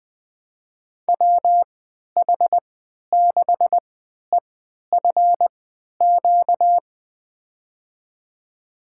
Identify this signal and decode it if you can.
Morse code, heard as a keyed tone, about 20 wpm, spaced as for 11 wpm, 700 Hz. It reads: WH6EFQ
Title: USA Call Signs - 3x Character Spacing - 20wpm